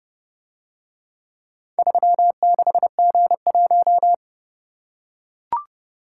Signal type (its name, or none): Morse code